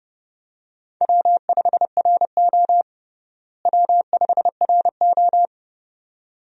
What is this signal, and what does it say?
Morse code, heard as a keyed tone, about 30 wpm, 700 Hz. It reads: W5RO W5RO
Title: USA Call Signs - 4 Characters - Repeated 2x - 30wpm